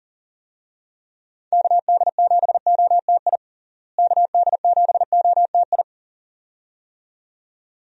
Morse code, heard as a keyed tone, about 40 wpm, 700 Hz. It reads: KD7OTI KD7OTI
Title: USA Call Signs - 6 Characters - Repeated 2x - 40wpm